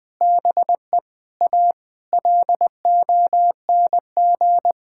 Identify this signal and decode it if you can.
Morse code, heard as a keyed tone, about 20 wpm, 700 Hz. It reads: BE A LONG
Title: Sentences with 5-Word Phrases - 20wpm